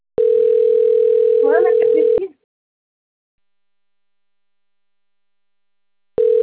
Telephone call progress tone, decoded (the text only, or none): ringback tone